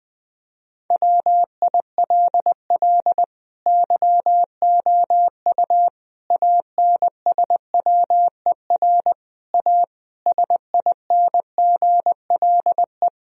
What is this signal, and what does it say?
Morse code, heard as a keyed tone, about 20 wpm, 700 Hz. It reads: WILL YOU ANSWER A SINGLE
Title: Sentences from Top 1000 Words - Review Part 4 - 20wpm